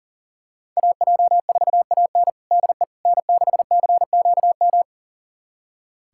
Morse code, heard as a keyed tone, about 40 wpm, 700 Hz. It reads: AJ4AN DE N6CQM
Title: USA Callsign DE Callsign - 5 Characters - 40wpm